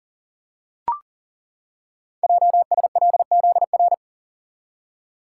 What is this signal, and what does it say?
Morse code, heard as a keyed tone, about 38 wpm, 690 Hz. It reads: JSLZR
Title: Code Groups - 5 Letters - 40wpm